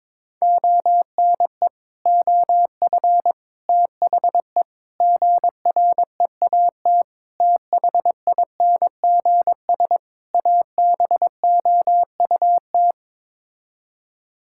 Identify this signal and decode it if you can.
Morse code, heard as a keyed tone, about 22 wpm, 700 Hz. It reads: ONE OF THE GREAT THINGS ABOUT
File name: practice / 22wpm / Sets of 6 Words - 22wpm